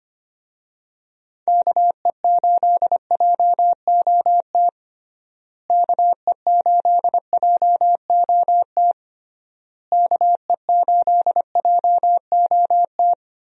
Morse code, heard as a keyed tone, about 25 wpm, 700 Hz. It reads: KE8JOT KE8JOT KE8JOT
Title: USA Call Signs - 6 Characters - Repeated 3x - 25wpm